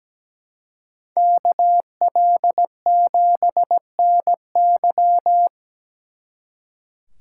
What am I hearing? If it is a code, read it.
Morse code, heard as a keyed tone, about 20 wpm, 700 Hz. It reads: KL7NY
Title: Speed Racing - USA Call Signs 17wpm